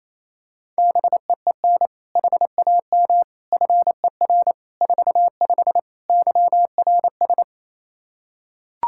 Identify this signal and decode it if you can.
Morse code, heard as a keyed tone, about 28 wpm, 700 Hz. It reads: BEEN HAM FER 45 YRS
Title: Practice QSO - Exchange 3 - More Info - 28wpm